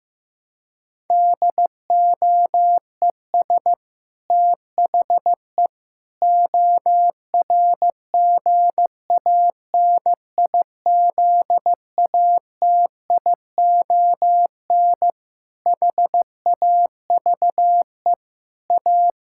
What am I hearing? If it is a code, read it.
Morse code, heard as a keyed tone, about 16 wpm, 700 Hz. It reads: DOES THE ORGANIZATION HAVE A
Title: Sentences from Top 1000 Words - Review Part 6 - 15wpm